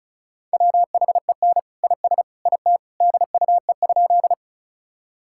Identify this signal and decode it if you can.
Morse code, heard as a keyed tone, about 36 wpm, 700 Hz. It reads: WHEN IS IT DUE?
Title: Sentences from Top 1000 Words - Review Part 6 - 35wpm